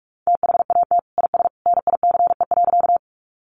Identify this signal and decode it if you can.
Morse code, heard as a keyed tone, about 44 wpm, 700 Hz. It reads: THAT IS NICE.
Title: Sentences from Top 1100 Words 45wpm